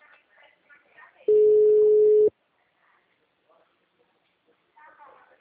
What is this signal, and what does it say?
Ringback tone (Europe, 425 Hz), with speech.